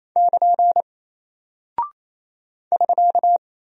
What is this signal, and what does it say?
Morse code, heard as a keyed tone, about 28 wpm, 700 Hz. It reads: <KN> <SK>
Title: Single Character (Letter:Number:Punctuation:Prosign) Alt 28wpm